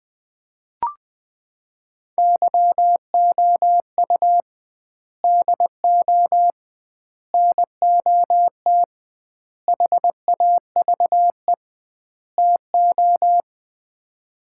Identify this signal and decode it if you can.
Morse code, heard as a keyed tone, about 20 wpm, 700 Hz. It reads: YOU DO NOT HAVE TO
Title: 2x Word-Spacing - Sets of 5 Words - 20wpm